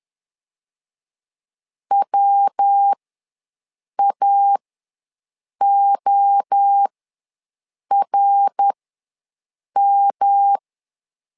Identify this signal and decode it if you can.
Morse code, heard as a keyed tone, about 10 wpm, spaced as for 6 wpm, 800 Hz. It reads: WAORM
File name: LEZIONI / 3 MWRO / 3 MWRO ws 10wpm 2310